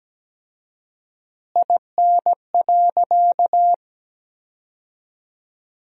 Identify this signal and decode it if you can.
Morse code, heard as a keyed tone, about 17 wpm, 700 Hz. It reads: IN.
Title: 4x Word-Spacing - Sentences from Top 100 Words - 17wpm